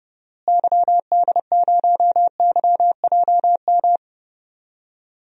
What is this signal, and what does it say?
Morse code, heard as a keyed tone, about 30 wpm, 700 Hz. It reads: YD0YJM
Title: DXCC-100 Call Signs - 6 Characters - Repeated 3x - 30wpm